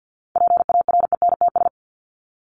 Morse code, heard as a keyed tone, about 52 wpm, 700 Hz. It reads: PARENTS
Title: ES:S Suffix - Top 100 Words with Base Words 50wpm